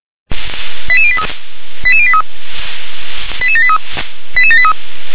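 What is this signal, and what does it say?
Five-tone selective call (ZVEI), heard as quick runs of five tones, tones of about 70 ms. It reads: 88083 88083 88783 88783
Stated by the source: Five-tone ZVEI selective call received on an SDR.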